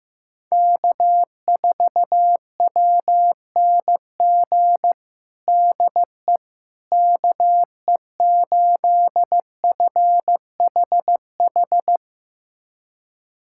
Morse code, heard as a keyed tone, about 15 wpm, 700 Hz. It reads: K4WNG DE KE8FHH
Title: USA Callsign DE Callsign 15wpm